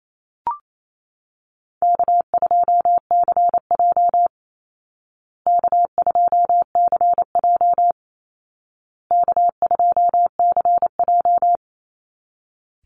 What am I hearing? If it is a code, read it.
Morse code, heard as a keyed tone, about 28 wpm, 700 Hz. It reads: K2CJ K2CJ K2CJ
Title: Easy Call Signs Repeated 3x - 28wpm